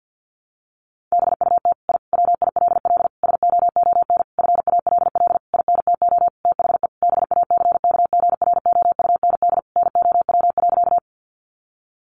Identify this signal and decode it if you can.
Morse code, heard as a keyed tone, about 50 wpm, 700 Hz. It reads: BUT I WILL SOON FALL INTO THE BACKGROUND NOW.
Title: Sentences from Top 1300 Words 50wpm